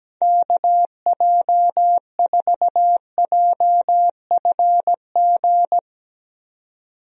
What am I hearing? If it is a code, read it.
Morse code, heard as a keyed tone, about 17 wpm, 700 Hz. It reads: KJ4JFG
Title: USA Call Signs - 6 Characters - 17wpm